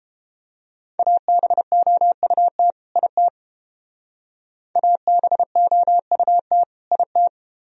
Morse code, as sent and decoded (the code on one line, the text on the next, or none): .- -... --- ..- - / .. - / .- -... --- ..- - / .. -
ABOUT IT ABOUT IT